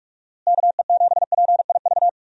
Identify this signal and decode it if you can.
Morse code, heard as a keyed tone, about 47 wpm, 700 Hz. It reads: KE7PIV
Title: USA Call Signs - 6 Characters - Repeated 3x - 45wpm